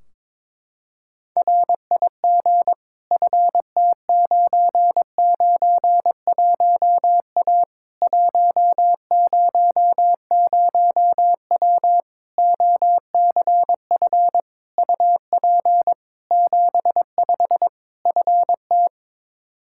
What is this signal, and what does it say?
Morse code, heard as a keyed tone, about 22 wpm, 700 Hz. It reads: RIG FT991A 100W OCF UP 75 FT
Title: K1USN SST QSO Practice 22wpm